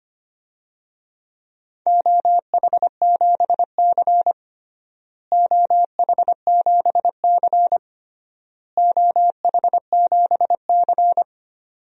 Morse code, heard as a keyed tone, about 25 wpm, 700 Hz. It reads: OH7C OH7C OH7C
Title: DXCC-100 Call Signs - 4 Characters - Repeated 3x - 25wpm